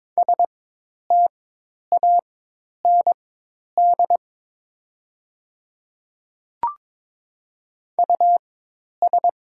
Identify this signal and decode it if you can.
Morse code, heard as a keyed tone, about 22 wpm, spaced as for 10 wpm, 700 Hz. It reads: STAND US
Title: 4x Character Spacing - Top 500 Words - 22wpm